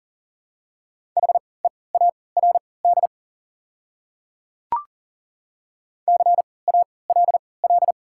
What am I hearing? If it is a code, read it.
Morse code, heard as a keyed tone, about 40 wpm, spaced as for 23 wpm, 690 Hz. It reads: HEARD CALL
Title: 3x Character Spacing - Top 500 Words - 40wpm